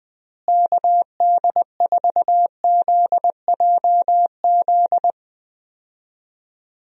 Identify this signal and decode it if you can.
Morse code, heard as a keyed tone, about 21 wpm, 700 Hz. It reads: KD4ZJZ
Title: USA Call Signs - 6 Characters - Repeated 2x - 20wpm